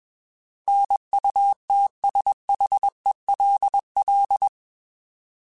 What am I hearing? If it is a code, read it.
Morse code, heard as a keyed tone, about 21 wpm, 800 Hz. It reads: NUTSHELL
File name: Alphabetical / Alphabetical mp3s / Ns @20